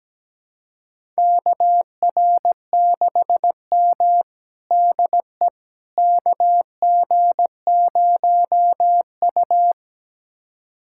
Morse code, heard as a keyed tone, about 17 wpm, 700 Hz. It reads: KR6M DE KG0U
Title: USA Callsign DE Callsign - 4 Characters - 17wpm